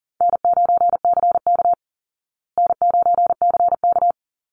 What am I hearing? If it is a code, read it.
Morse code, heard as a keyed tone, about 40 wpm, 700 Hz. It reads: N9CK N9CK
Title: Easy Call Signs Repeated 3x - 40wpm